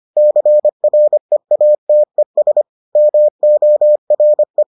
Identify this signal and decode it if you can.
Morse code, heard as a keyed tone, about 25 wpm, 600 Hz. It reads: CREATES MORE